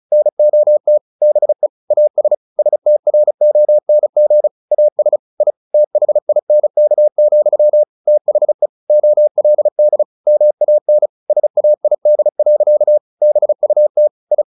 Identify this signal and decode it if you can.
Morse code, heard as a keyed tone, about 34 wpm, 600 Hz. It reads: NOT BE AS STRONG AS I THINK, THE OLD MAN SAID. BUT I
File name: the old man and the sea / the old man and the sea 1 35wpm